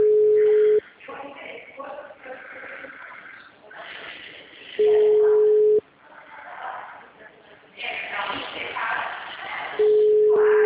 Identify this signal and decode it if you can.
Ringback tone (Europe, 425 Hz), with speech.
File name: out- -2017- - - .62062